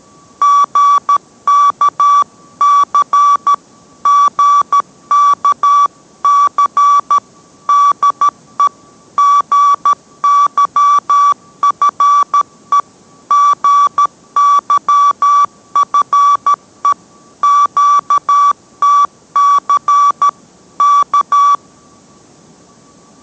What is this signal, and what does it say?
Morse code, heard as a keyed tone, about 13 wpm, 1150 Hz. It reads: GKC GKC DE GYFE GYFE QTC K